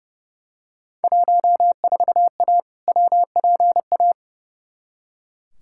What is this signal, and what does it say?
Morse code, heard as a keyed tone, about 31 wpm, 700 Hz. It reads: 14A WPA
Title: Field Day Exchanges Encore - 30wpm